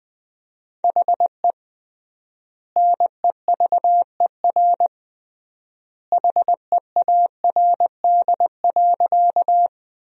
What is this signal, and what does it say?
Morse code, heard as a keyed tone, about 20 wpm, 700 Hz. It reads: HE NEVER HEARD.
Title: 3x Word-Spacing - Sentences from Top 400 Words - 20wpm